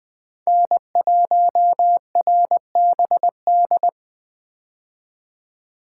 Morse code, heard as a keyed tone, about 21 wpm, 700 Hz. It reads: N1RBD
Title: K1USN SST QSO Practice - 20wpm